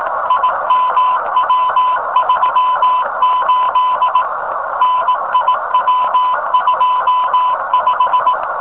Morse code, heard as a keyed tone, about 18 wpm, 1000 Hz, with noise. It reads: IMW38 NIW25
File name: rs15 telem